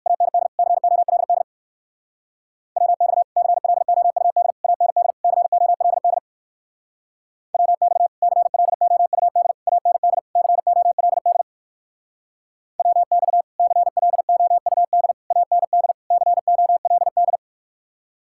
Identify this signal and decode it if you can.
Morse code, heard as a keyed tone, about 54 wpm, 700 Hz. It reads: AND COLD WX CLOUD AND COLD WX CLOUD AND COLD WX CLOUD AND COLD
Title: Speed-Racing Common QSO Phrases - 45wpm